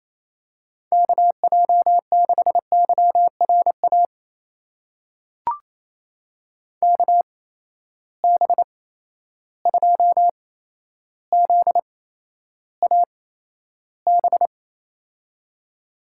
Morse code, heard as a keyed tone, about 28 wpm, spaced as for 12 wpm, 700 Hz. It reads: KJ6YRA KB2ZAB
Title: USA Call Signs - 6 Characters - 8x Character Spacing - 28wpm